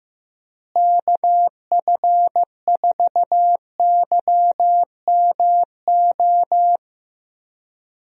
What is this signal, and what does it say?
Morse code, heard as a keyed tone, about 15 wpm, 700 Hz. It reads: KF4YMO
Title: USA Call Signs - 6 Characters - Repeated 3x - 15wpm